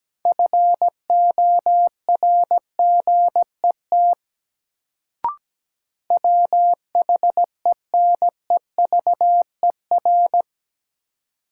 Morse code, heard as a keyed tone, about 17 wpm, 700 Hz. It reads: FORGET WHENEVER
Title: Sentences from Top 1200 Words - New Word Drill - 17wpm